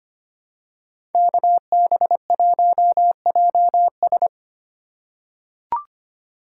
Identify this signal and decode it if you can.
Morse code, heard as a keyed tone, about 25 wpm, 700 Hz. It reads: KB1JS
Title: USA Call Signs - 5 Characters - 4x Character Spacing - 25wpm